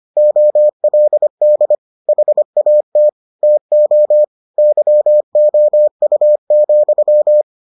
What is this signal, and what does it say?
Morse code, heard as a keyed tone, about 25 wpm, 600 Hz. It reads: OLD HAT TO YOU,